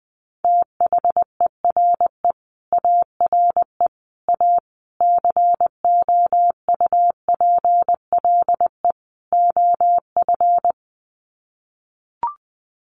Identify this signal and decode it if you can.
Morse code, heard as a keyed tone, about 20 wpm, 700 Hz. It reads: THERE ARE A COUPLE OF
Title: Sets of 5 Words Encore 20wpm